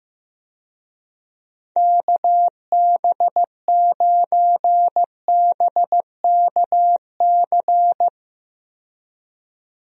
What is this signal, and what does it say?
Morse code, heard as a keyed tone, about 15 wpm, 700 Hz. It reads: KB9BKC
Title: USA Call Signs - 6 Characters - 15wpm